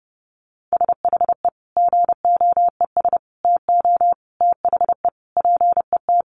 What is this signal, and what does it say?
Morse code, heard as a keyed tone, about 30 wpm, 700 Hz. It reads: SHE GOES TO THE PET
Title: Easy Phrases 30wpm